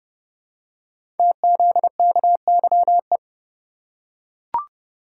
Morse code, heard as a keyed tone, about 30 wpm, 700 Hz. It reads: TZKYE
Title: Code Groups - 5 Letters - 30wpm